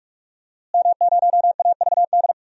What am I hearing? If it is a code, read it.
Morse code, heard as a keyed tone, about 46 wpm, 700 Hz. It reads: M0AVD
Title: DXCC-100 Call Signs - 5 Characters - Repeated 2x - 45wpm